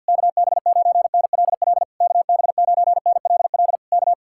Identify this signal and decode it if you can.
Morse code, heard as a keyed tone, about 67 wpm, spaced as for 58 wpm, 700 Hz. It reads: KB9NLL KB9NLL K
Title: Speed-Racing Common QSO Phrases - 50wpm